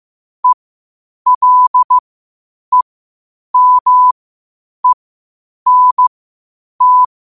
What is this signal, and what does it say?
Morse code, heard as a keyed tone, about 15 wpm, spaced as for 8 wpm, 1000 Hz. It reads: ELEMENT
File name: qotdfiles / 8wpm